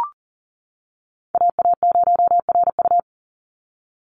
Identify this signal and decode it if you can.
Morse code, heard as a keyed tone, about 38 wpm, 690 Hz. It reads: AA0RU
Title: USA Call Signs 40wpm